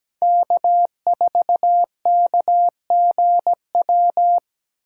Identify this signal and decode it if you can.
Morse code, heard as a keyed tone, about 20 wpm, 700 Hz. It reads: K4KGW
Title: Speed Racing - USA 1x3 Call Signs 17wpm